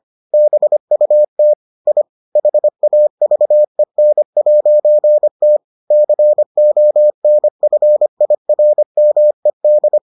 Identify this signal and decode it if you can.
Morse code, heard as a keyed tone, about 25 wpm, 600 Hz. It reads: BUT I HAVEN'T CONFIRMED